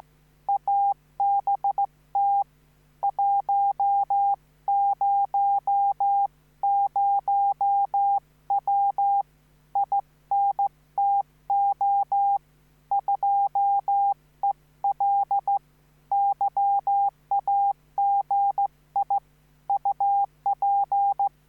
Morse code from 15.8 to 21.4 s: YAGI UP